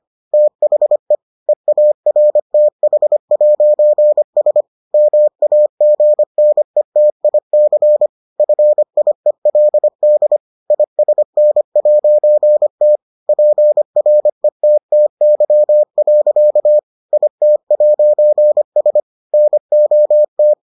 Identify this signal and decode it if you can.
Morse code, heard as a keyed tone, about 25 wpm, 600 Hz. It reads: THE EARTH'S MAGNETIC FIELD ISN'T PRETTY. IT'S NOT